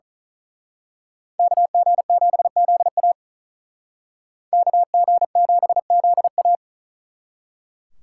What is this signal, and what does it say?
Morse code, heard as a keyed tone, about 36 wpm, 700 Hz. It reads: KG7ZA KG7ZA
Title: Speed Racing - USA Call Signs 35wpm